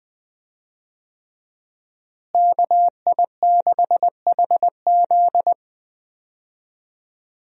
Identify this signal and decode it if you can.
Morse code, heard as a keyed tone, about 20 wpm, 700 Hz. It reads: KI6HZ
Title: USA Call Signs - 5 Characters - Repeated 3x - 20wpm